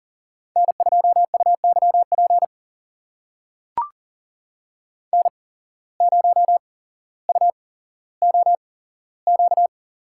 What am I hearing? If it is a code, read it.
Morse code, heard as a keyed tone, about 40 wpm, spaced as for 17 wpm, 700 Hz. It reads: N2UYP N0UOQ
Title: USA Call Signs - 8x Character Spacing - 40wpm